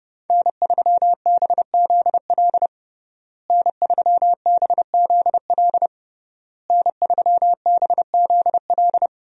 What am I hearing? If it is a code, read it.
Morse code, heard as a keyed tone, about 30 wpm, 700 Hz. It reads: N3BZL N3BZL N3BZL
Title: USA Call Signs - 5 Characters - Repeated 3x - 30wpm